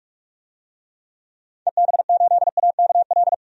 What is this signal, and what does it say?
Morse code, heard as a keyed tone, about 44 wpm, 700 Hz. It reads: EB8AKL
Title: DXCC-100 Call Signs - 6 Characters - 45wpm